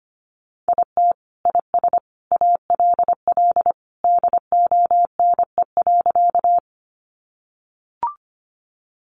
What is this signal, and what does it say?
Morse code, heard as a keyed tone, about 25 wpm, 700 Hz. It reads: IT IS ALL DONE.
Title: Sentences from Top 500 Words 25wpm